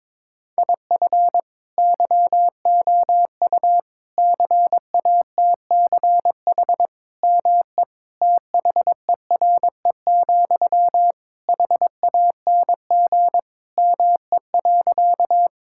Morse code, heard as a keyed tone, about 22 wpm, 700 Hz. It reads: IF YOU CATCH ME THERE, HANG ME.
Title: Sentences from Top 1600 Words 22wpm